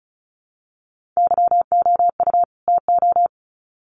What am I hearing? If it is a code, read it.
Morse code, heard as a keyed tone, about 34 wpm, 700 Hz. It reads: YOU TO
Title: Sets of 2 Words 35wpm